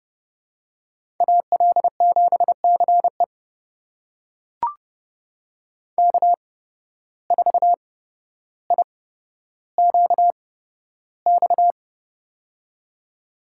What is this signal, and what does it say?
Morse code, heard as a keyed tone, about 30 wpm, spaced as for 13 wpm, 700 Hz. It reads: AL7CE K4IQX
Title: USA Call Signs - 5 Characters - 8x Character Spacing - 30wpm